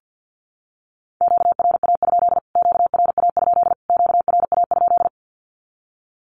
Morse code, heard as a keyed tone, about 50 wpm, 700 Hz. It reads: QRA? QRA? QRA?
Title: Q-Codes Repeated 3x 50wpm